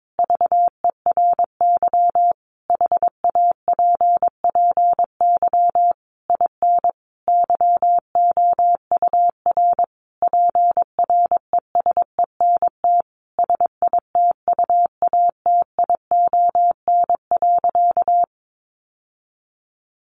Morse code, heard as a keyed tone, about 22 wpm, 700 Hz. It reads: VERY HAPPY IN YOUR PRESENT SITUATION.
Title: Sentences from Top 1200 Words 22wpm